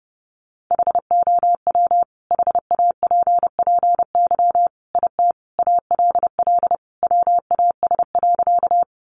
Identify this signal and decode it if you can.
Morse code, heard as a keyed tone, about 30 wpm, 700 Hz. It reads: HOW HAPPY IT ALL WAS.
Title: Sentences from Top 1000 Words 30wpm